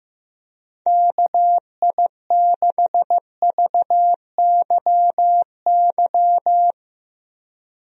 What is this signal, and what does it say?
Morse code, heard as a keyed tone, about 16 wpm, 700 Hz. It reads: KI6VYY
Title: USA Call Signs - 6 Characters - Repeated 3x - 15wpm